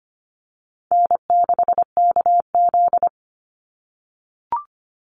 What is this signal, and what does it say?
Morse code, heard as a keyed tone, about 25 wpm, 700 Hz. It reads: N6KZ
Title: NA SOTA Chaser Callsigns 25wpm